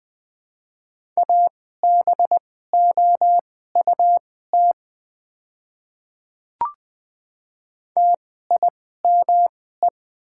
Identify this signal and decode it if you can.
Morse code, heard as a keyed tone, about 20 wpm, spaced as for 14 wpm, 700 Hz. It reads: ABOUT TIME
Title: 2x Character Spacing - Top 100 Words - 20wpm